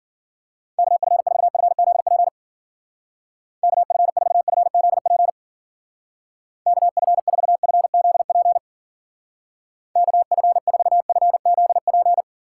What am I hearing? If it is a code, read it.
Morse code, heard as a keyed tone, about 52 wpm, 700 Hz. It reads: KF4FZP KF4FZP KF4FZP KF4FZP